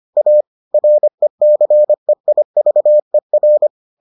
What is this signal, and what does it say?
Morse code, heard as a keyed tone, about 25 wpm, 600 Hz. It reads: A RECEIVER